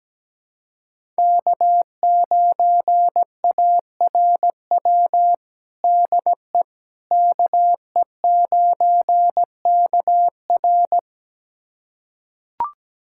Morse code, heard as a keyed tone, about 17 wpm, 700 Hz. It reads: K9ARW DE KE9KR
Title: USA Callsign DE Callsign - 5 Characters - 17wpm